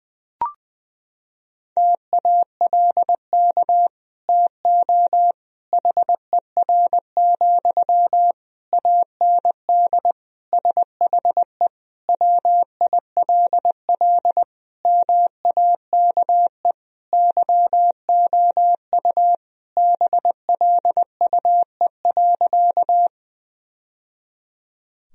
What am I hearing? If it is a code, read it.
Morse code, heard as a keyed tone, about 20 wpm, 700 Hz. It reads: TALK TO HER, AND SHE WILL MAKE YOU BLUE.
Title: Sentences from Top 1000 Words - Review Part 3 - 20wpm